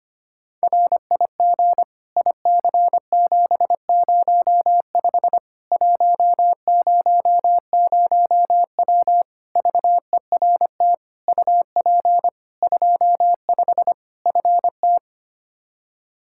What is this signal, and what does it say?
Morse code, heard as a keyed tone, about 25 wpm, 700 Hz. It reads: RIG IC705 100W VERT UP 25 FT
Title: K1USN SST QSO Practice 25wpm